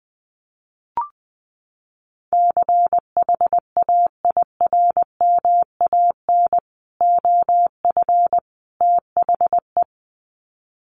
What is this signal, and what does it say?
Morse code, heard as a keyed tone, about 20 wpm, 700 Hz. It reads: CHAIRMAN OF THE
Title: Sets of 3 Words 20wpm